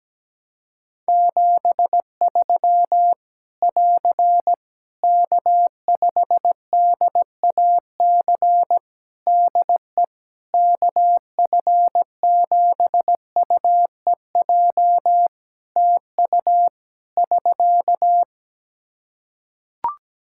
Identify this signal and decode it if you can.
Morse code, heard as a keyed tone, about 17 wpm, 700 Hz. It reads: 73 <AR> K5DAC DE KF7UEJ TU <SK>
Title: Practice QSO - Exchange 4 - Ending - 17wpm